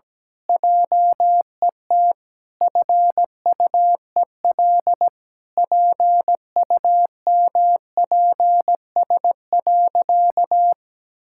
Morse code, heard as a keyed tone, about 17 wpm, 700 Hz. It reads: JET FUEL PUMPS.
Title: Pangram Challenge v2 - 17wpm